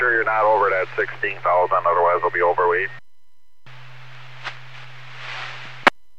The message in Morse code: none

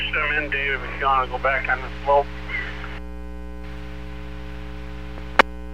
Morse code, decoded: none

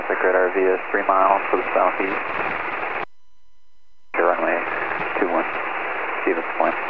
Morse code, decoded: none

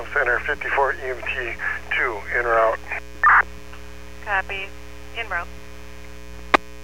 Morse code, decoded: none

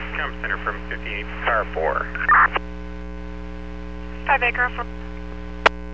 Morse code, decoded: none

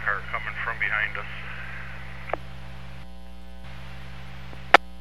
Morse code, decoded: none

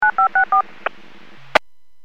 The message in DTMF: 9564

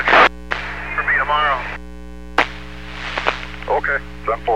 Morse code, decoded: none

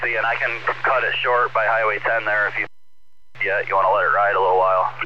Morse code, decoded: none

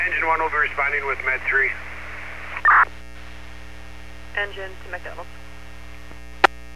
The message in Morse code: none

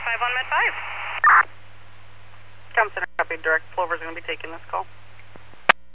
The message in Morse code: none

